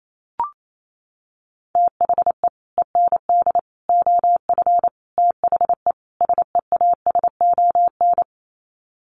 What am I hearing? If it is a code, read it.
Morse code, heard as a keyed tone, about 28 wpm, 700 Hz. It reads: THE END OF THE SEASON
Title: Sets of 5 Words Encore 28wpm